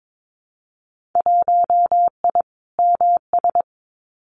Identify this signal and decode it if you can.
Morse code, heard as a keyed tone, about 22 wpm, 700 Hz. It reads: 1I MS
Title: Winter Field Day Exchanges Encore 22wpm